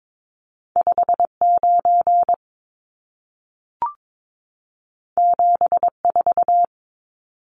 Morse code, heard as a keyed tone, about 22 wpm, 700 Hz. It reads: 59 74
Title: Numbers - 2-Digit 22wpm